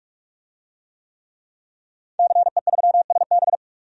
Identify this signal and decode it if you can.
Morse code, heard as a keyed tone, about 46 wpm, 700 Hz. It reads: CE3SB
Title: DXCC-100 Call Signs - 5 Characters - Repeated 2x - 45wpm